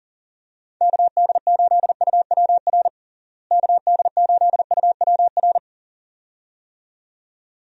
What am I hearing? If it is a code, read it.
Morse code, heard as a keyed tone, about 40 wpm, 700 Hz. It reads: KD8UWR KD8UWR
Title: USA Call Signs - 6 Characters - Repeated 2x - 40wpm